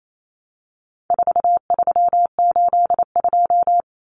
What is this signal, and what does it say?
Morse code, heard as a keyed tone, about 28 wpm, 700 Hz. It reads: 4382
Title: Common Cut Numbers - 4-Digit 28wpm